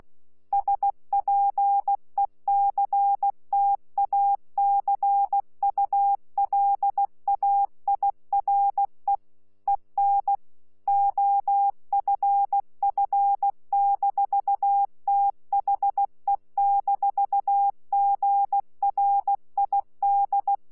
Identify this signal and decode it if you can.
Morse code, heard as a keyed tone, about 16 wpm, 800 Hz. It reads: SPECTACULAIRE EN OFF-THE-GRID
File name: Z PI4VRZ / PI4VRZ 14-jun-2025